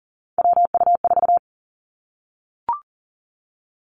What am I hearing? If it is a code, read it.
Morse code, heard as a keyed tone, about 38 wpm, 700 Hz. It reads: WU4
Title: Three Challenging Letter-Number Combinations 40wpm